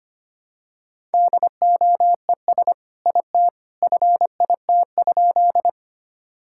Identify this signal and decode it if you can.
Morse code, heard as a keyed tone, about 25 wpm, 700 Hz. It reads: DOES IT FIT?
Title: Sentences from Top 1000 Words - Review Part 5 - 25wpm